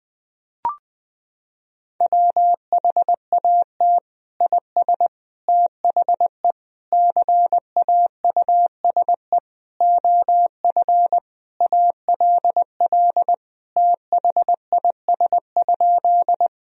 Morse code, heard as a keyed tone, about 20 wpm, 700 Hz. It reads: WHAT IS THE CAUSE OF ALL THIS?
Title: Sentences from Top 1000 Words - Review Part 1 - 20wpm